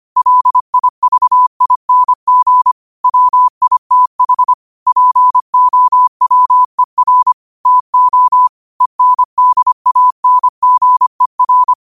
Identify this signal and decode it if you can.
Morse code, heard as a keyed tone, about 25 wpm, 1000 Hz. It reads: LIVING WITH POWER TO ENDANGER
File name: qotdfiles / 25wpm